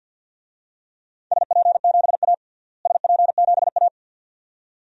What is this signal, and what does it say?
Morse code, heard as a keyed tone, about 47 wpm, 700 Hz. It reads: SP7A SP7A
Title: DXCC-100 Call Signs - 4 Characters - Repeated 2x - 50wpm